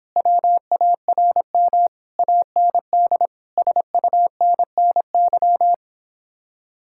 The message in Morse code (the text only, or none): WARM AND SUNNY